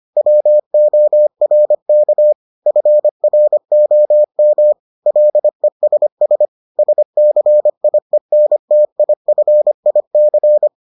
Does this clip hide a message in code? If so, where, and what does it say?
Yes: Morse code from 0.2 to 10.7 s, heard as a keyed tone, about 25 wpm, 600 Hz. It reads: WORK FROM LESS SCIENTIFIC